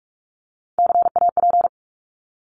Morse code, heard as a keyed tone, about 48 wpm, 700 Hz. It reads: CAP
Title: Top 1501-1750 Words 45wpm